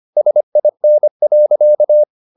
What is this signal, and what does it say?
Morse code, heard as a keyed tone, about 25 wpm, 600 Hz. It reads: SIN.